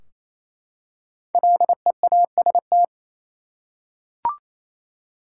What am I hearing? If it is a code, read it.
Morse code, heard as a keyed tone, about 28 wpm, 700 Hz. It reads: LEAST
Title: Five Letter Words - 28wpm